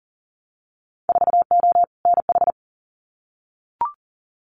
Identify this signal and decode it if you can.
Morse code, heard as a keyed tone, about 40 wpm, 700 Hz. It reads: 4O NH
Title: Winter Field Day Exchanges Encore 40wpm